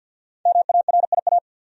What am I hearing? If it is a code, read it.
Morse code, heard as a keyed tone, about 53 wpm, 700 Hz. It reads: MARIA
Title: USA Common Names - 50wpm